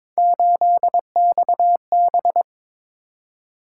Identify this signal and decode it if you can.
Morse code, heard as a keyed tone, about 23 wpm, 700 Hz. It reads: 8XB
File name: practice / 22wpm / Three Letter-Number Combinations - 22wpm